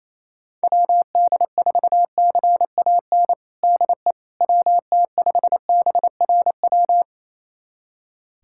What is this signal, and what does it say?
Morse code, heard as a keyed tone, about 28 wpm, 700 Hz. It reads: WD4CAN DE WT5BRW
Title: USA Callsign DE Callsign 28wpm